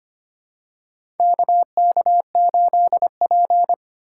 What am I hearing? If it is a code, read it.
Morse code, heard as a keyed tone, about 25 wpm, 700 Hz. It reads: KK8P
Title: USA Call Signs - 4 Characters - 3x Character Spacing - 25wpm